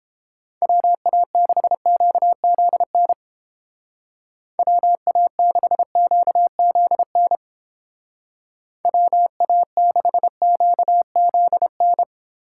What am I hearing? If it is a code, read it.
Morse code, heard as a keyed tone, about 31 wpm, 700 Hz. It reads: WA6QZN WA6QZN WA6QZN